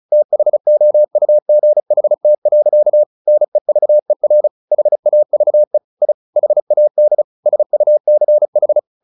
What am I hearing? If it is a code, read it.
Morse code, heard as a keyed tone, about 34 wpm, 600 Hz. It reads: THOUGHT. NEVER HAVE I HAD SUCH
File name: the old man and the sea / the old man and the sea 1 35wpm